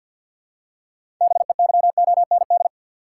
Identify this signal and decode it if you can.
Morse code, heard as a keyed tone, about 50 wpm, 700 Hz. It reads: BEYOND
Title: Six Letter Words - 50wpm